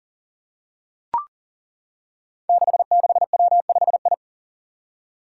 Morse code, heard as a keyed tone, about 40 wpm, 700 Hz. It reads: 66W5I
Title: Five Challenging Letter-Number Combinations 40wpm